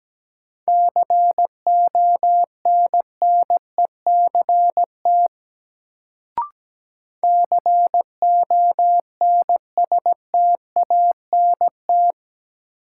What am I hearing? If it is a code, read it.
Morse code, heard as a keyed tone, about 17 wpm, 700 Hz. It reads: CONNECT CONSTANT
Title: Sentences from Top 1500 Words - New Word Drill Alt - 17wpm